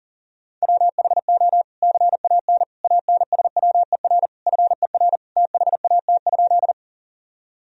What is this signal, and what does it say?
Morse code, heard as a keyed tone, about 40 wpm, 700 Hz. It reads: WHO CAN ANSWER FER THAT?
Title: Sentences from Top 300 Words and Abbreviations 40wpm